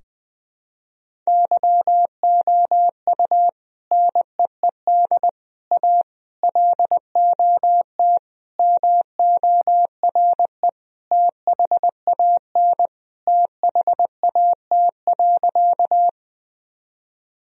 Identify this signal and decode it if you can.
Morse code, heard as a keyed tone, about 20 wpm, 700 Hz. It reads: YOU NEED A LOT MORE THAN THAT.
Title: Sentences from Top 1000 Words - Review Part 3 - 20wpm